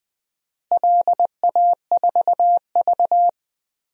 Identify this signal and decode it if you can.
Morse code, heard as a keyed tone, about 20 wpm, 700 Hz. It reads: LA4V
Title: DXCC-100 Call Signs - 4 Characters - Repeated 3x - 20wpm